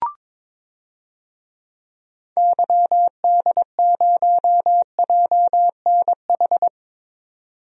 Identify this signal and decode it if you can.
Morse code, heard as a keyed tone, about 22 wpm, 700 Hz. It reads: YD0JNH
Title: DXCC-100 Call Signs - 6 Characters - 22wpm